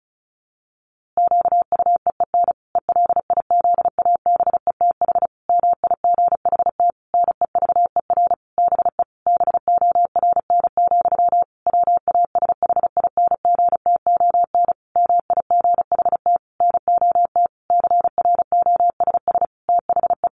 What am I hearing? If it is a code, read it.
Morse code, heard as a keyed tone, about 34 wpm, 700 Hz. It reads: QUEEN ELIZABETH MIGHT NEVER BE BORN, WASHINGTON MIGHT NOT CROSS THE
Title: A Sound Of Thunder 35wpm